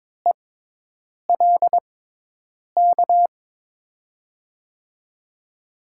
Morse code, heard as a keyed tone, about 22 wpm, spaced as for 8 wpm, 700 Hz. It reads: ELK
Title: USA Call Signs - 6x Character Spacing - 22wpm